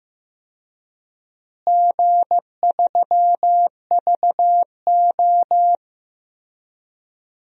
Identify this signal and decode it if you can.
Morse code, heard as a keyed tone, about 15 wpm, 700 Hz. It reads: G3VO
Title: DXCC-100 Call Signs - 4 Characters - Repeated 2x - 15wpm